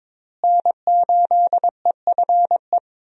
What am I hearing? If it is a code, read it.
Morse code, heard as a keyed tone, about 22 wpm, 700 Hz. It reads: N8EFE
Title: USA Call Signs - 5 Characters - Repeated 3x - 22wpm